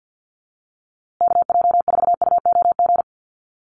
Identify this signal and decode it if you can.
Morse code, heard as a keyed tone, about 50 wpm, 700 Hz. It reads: KJ4UOG
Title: USA Call Signs Repeated 3x 50wpm